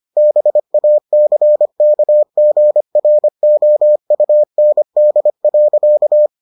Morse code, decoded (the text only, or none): BACKGROUND.